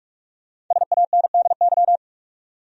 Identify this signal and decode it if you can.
Morse code, heard as a keyed tone, about 46 wpm, 700 Hz. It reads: SANDY
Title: USA Common Names - Max 5 Letters - 45wpm